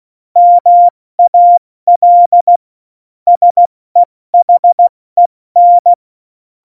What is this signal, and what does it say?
Morse code, heard as a keyed tone, about 16 wpm, 700 Hz. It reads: MAL SEHEN
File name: Morsekurs / Witze / Witz 18. September 2018, 70 F80